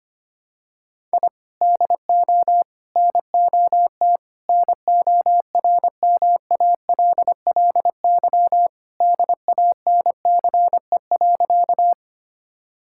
Morse code, heard as a keyed tone, about 25 wpm, 700 Hz. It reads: I DO NOT NORMALLY DANCE.
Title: Sentences from Top 1000 Words - Review Part 6 - 25wpm